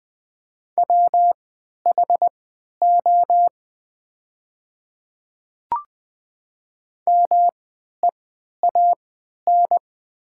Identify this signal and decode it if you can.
Morse code, heard as a keyed tone, about 20 wpm, spaced as for 11 wpm, 700 Hz. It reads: WHO MEAN
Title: 3x Character Spacing - Top 200 Words - 20wpm